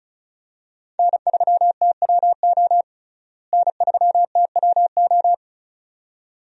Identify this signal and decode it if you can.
Morse code, heard as a keyed tone, about 34 wpm, 700 Hz. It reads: N3TWO N3TWO
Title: USA Call Signs - 5 Characters - Repeated 2x - 35wpm